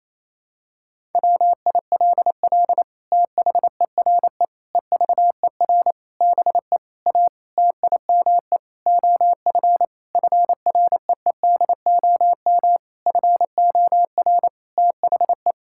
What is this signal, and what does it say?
Morse code, heard as a keyed tone, about 28 wpm, 700 Hz. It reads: WILL THERE EVER BE A TIME OF FREEDOM FOR THE
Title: Sentences from Top 1000 Words - Review Part 5 - 28wpm